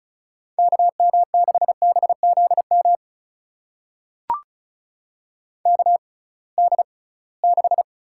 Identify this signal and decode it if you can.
Morse code, heard as a keyed tone, about 36 wpm, spaced as for 23 wpm, 700 Hz. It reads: KM6BZM KD6
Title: USA Call Signs - 6 Characters - 6x Character Spacing - 35wpm